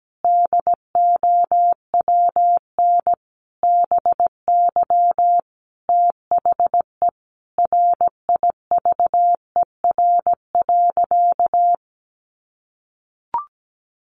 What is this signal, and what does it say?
Morse code, heard as a keyed tone, about 17 wpm, 700 Hz. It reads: DOWN BY THE RIVER.
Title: Sentences from Top 1200 Words 17wpm